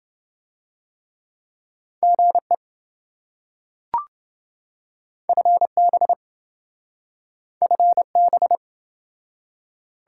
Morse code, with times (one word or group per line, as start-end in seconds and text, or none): 2.0-2.5 GE
5.3-6.1 FB
7.6-8.6 FB